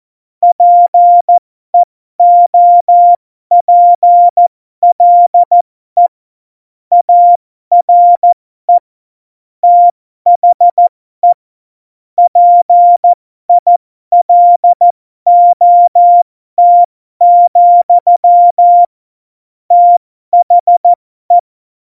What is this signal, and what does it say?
Morse code, heard as a keyed tone, about 14 wpm, spaced as for 12 wpm, 700 Hz. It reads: PEOPLE ARE THE PILOT, THE
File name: Morsekurs / Witze